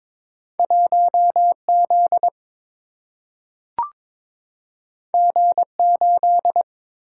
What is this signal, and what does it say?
Morse code, heard as a keyed tone, about 21 wpm, 700 Hz. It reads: 1 Z G 8
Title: Single-Letter Single-Number 22wpm